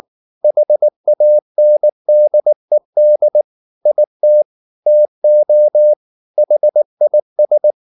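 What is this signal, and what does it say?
Morse code, heard as a keyed tone, about 19 wpm, 600 Hz. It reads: HANDED IT TO HIS
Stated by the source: Morse code at 19 wpm from CW med Gnister, reading Ken Follett's novel Never in English.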